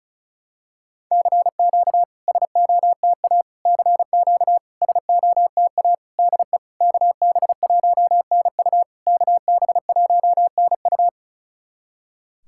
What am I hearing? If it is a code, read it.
Morse code, heard as a keyed tone, about 36 wpm, 690 Hz. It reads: CQ SOTA CQ SOTA DE KB1NU KB1NU
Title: SOTA QSO Practice 35wpm